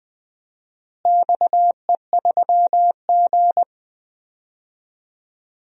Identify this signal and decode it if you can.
Morse code, heard as a keyed tone, about 20 wpm, 700 Hz. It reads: XE3G
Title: DXCC-100 Call Signs - 4 Characters - Repeated 2x - 20wpm